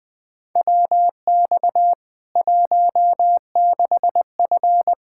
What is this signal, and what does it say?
Morse code, heard as a keyed tone, about 21 wpm, 700 Hz. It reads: WX 16F
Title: K1USN SST QSO Practice 20wpm